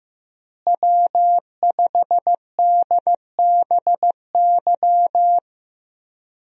Morse code, heard as a keyed tone, about 15 wpm, 700 Hz. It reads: W5DBY
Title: USA Call Signs - 5 Characters - Repeated 2x - 15wpm